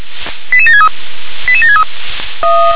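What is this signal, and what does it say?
Five-tone selective call (ZVEI), heard as quick runs of five tones, tones of about 70 ms. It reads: 88763 88763
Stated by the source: Received on SDR, ZVEI five-tone call, address 88771.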